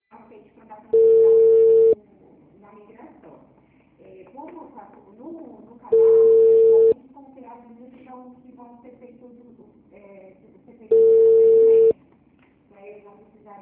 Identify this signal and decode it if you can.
Ringback tone (China, 450 Hz), with speech.